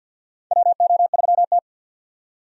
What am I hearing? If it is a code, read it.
Morse code, heard as a keyed tone, about 50 wpm, 700 Hz. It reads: WO3T
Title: USA Call Signs - 4 Characters - 50wpm